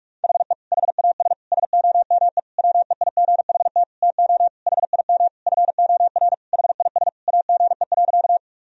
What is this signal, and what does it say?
Morse code, heard as a keyed tone, about 42 wpm, 700 Hz. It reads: HE HAS SOME WEIGHT TO HIM FOR HIS AGE.
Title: Sentences from Top 1000 Words - Review Part 3 - 45wpm